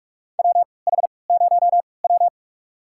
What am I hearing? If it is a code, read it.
Morse code, heard as a keyed tone, about 46 wpm, spaced as for 25 wpm, 700 Hz. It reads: WH0W
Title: USA Call Signs - 4 Characters - 3x Character Spacing - 45wpm